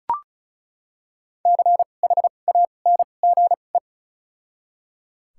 Morse code, heard as a keyed tone, about 34 wpm, spaced as for 25 wpm, 700 Hz. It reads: CHANGE
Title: 2x Character Spacing - Top 200 Words - 35wpm